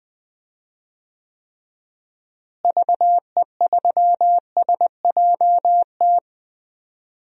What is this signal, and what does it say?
Morse code, heard as a keyed tone, about 20 wpm, 700 Hz. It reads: VE3SJT
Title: DXCC-100 Call Signs - 6 Characters - Repeated 3x - 20wpm